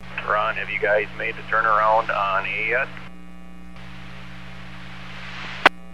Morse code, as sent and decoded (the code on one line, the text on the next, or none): none